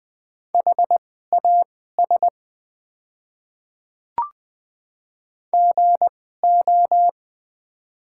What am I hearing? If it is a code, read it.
Morse code, heard as a keyed tone, about 21 wpm, spaced as for 15 wpm, 700 Hz. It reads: HAS GO
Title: 2x Character Spacing - Top 300 Words - 20wpm